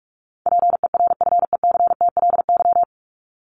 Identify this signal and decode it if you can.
Morse code, heard as a keyed tone, about 44 wpm, 700 Hz. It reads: PERFECTLY
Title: LY Suffix - Top 100 Words with Base Words 45wpm